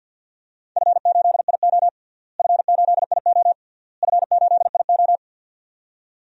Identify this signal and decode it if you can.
Morse code, heard as a keyed tone, about 51 wpm, 700 Hz. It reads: F8IO F8IO F8IO
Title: DXCC-100 Call Signs - 4 Characters - Repeated 3x - 50wpm